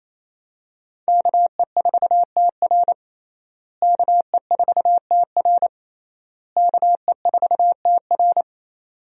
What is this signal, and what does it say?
Morse code, heard as a keyed tone, about 28 wpm, 700 Hz. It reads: KE4TR KE4TR KE4TR
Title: USA Call Signs - 5 Characters - Repeated 3x - 28wpm